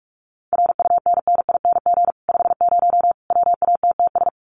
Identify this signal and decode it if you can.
Morse code, heard as a keyed tone, about 44 wpm, 700 Hz. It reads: RUNNING 50 WATTS
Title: Common QSO Phrases 45wpm